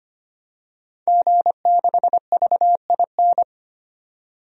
Morse code, heard as a keyed tone, about 25 wpm, 700 Hz. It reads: G6VIN
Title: DXCC-100 Call Signs - 5 Characters - 25wpm